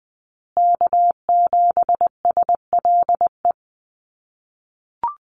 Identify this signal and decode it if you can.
Morse code, heard as a keyed tone, about 20 wpm, 700 Hz. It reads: K7SLE
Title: USA Call Signs Repeated 3x 20wpm